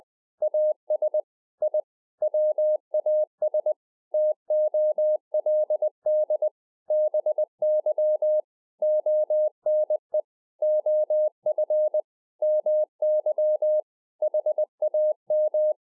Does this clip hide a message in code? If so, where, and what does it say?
Yes: Morse code from 0.4 to 15.7 s, heard as a keyed tone, about 20 wpm, 610 Hz. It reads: AS I WAS TOLD BY ONE OF MY HAM